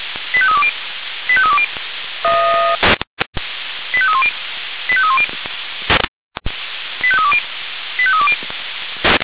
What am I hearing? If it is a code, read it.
Five-tone selective call (ZVEI), heard as quick runs of five tones, tones of about 70 ms. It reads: 85320 85320 85310 85310 85320 85320
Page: ZVEI